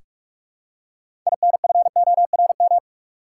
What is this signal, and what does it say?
Morse code, heard as a keyed tone, about 45 wpm, 700 Hz. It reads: INFORM
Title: Six Letter Words - 45wpm